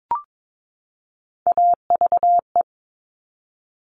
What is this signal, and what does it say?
Morse code, heard as a keyed tone, about 22 wpm, 700 Hz. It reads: AVE
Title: Top Three Letter Combinations in Words 22wpm